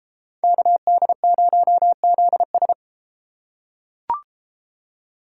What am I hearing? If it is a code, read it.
Morse code, heard as a keyed tone, about 33 wpm, 700 Hz. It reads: KD0ZS